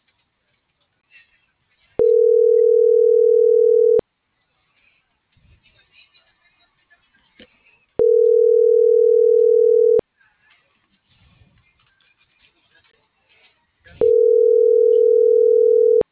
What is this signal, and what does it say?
Ringback tone (North America), 440 + 480 Hz.